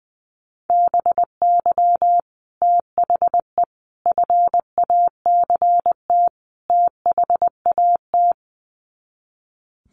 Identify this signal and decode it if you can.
Morse code, heard as a keyed tone, about 20 wpm, 700 Hz. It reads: BY THE FACT THAT
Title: Sets of 4 Words 20wpm